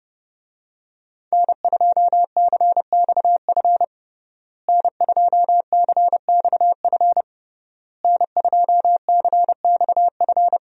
Morse code, heard as a keyed tone, about 30 wpm, 700 Hz. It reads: N2CXF N2CXF N2CXF
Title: USA Call Signs - 5 Characters - Repeated 3x - 30wpm